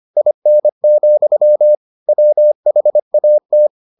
Morse code, heard as a keyed tone, about 25 wpm, 600 Hz. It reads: IN, WHAT